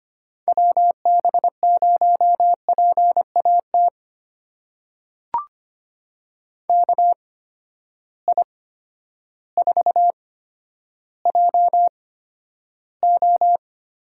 Morse code, heard as a keyed tone, about 25 wpm, spaced as for 11 wpm, 700 Hz. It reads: WB0PAT KI4JO
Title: USA Call Signs - 6 Characters - 8x Character Spacing - 25wpm